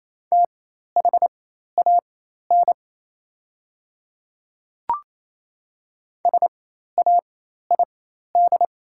Morse code, heard as a keyed tone, about 28 wpm, spaced as for 13 wpm, 700 Hz. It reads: THAN SAID
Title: 4x Character Spacing - Top 100 Words - 28wpm